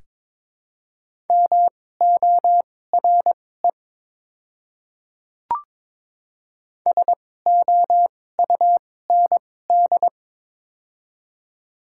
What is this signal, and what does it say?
Morse code, heard as a keyed tone, about 22 wpm, spaced as for 16 wpm, 700 Hz. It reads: MORE SOUND
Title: 2x Character Spacing - Top 200 Words - 22wpm